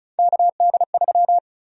Morse code, heard as a keyed tone, about 36 wpm, 700 Hz. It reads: KD3
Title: USA Call Sign Prefixes - 35wpm